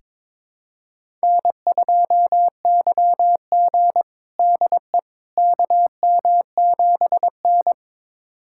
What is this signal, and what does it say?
Morse code, heard as a keyed tone, about 21 wpm, 700 Hz. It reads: N2YG DE KM7N
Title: USA Callsign DE Callsign - 4 Characters - 22wpm